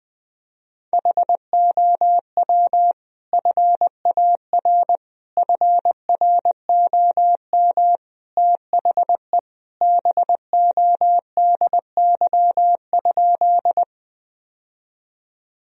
Morse code, heard as a keyed tone, about 20 wpm, 700 Hz. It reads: HOW FAR FROM THE BODY?
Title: Sentences from Top 1000 Words - Review Part 2 - 20wpm